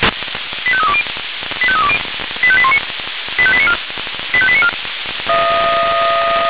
Five-tone selective call (ZVEI), heard as quick runs of five tones, tones of about 70 ms. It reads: 85320 85320 85819 85894 85894
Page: SDR/ZVEI